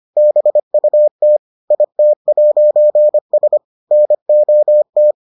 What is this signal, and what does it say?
Morse code, heard as a keyed tone, about 25 wpm, 600 Hz. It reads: BUT IT'S NOT